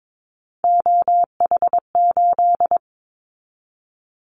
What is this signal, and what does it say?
Morse code, heard as a keyed tone, about 22 wpm, 700 Hz. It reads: OH8
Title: Callsign Prefixes for Top 25 DXCC - 22wpm